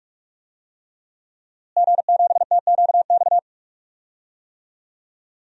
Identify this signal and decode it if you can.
Morse code, heard as a keyed tone, about 46 wpm, 700 Hz. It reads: G7TQX
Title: DXCC-100 Call Signs - 5 Characters - Repeated 3x - 45wpm